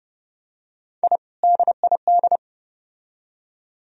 Morse code, heard as a keyed tone, about 28 wpm, 700 Hz. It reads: I DID